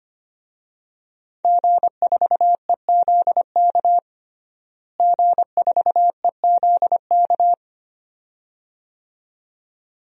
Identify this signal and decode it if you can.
Morse code, heard as a keyed tone, about 25 wpm, 700 Hz. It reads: G4EZK G4EZK
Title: DXCC-100 Call Signs - 5 Characters - Repeated 2x - 25wpm